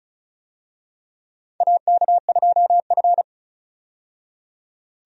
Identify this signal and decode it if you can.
Morse code, heard as a keyed tone, about 36 wpm, 700 Hz. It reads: AK2F
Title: USA Call Signs - 4 Characters - 35wpm